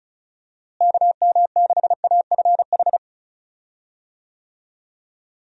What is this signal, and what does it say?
Morse code, heard as a keyed tone, about 36 wpm, 700 Hz. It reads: KM6AFH
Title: USA Call Signs - 6 Characters - Repeated 2x - 35wpm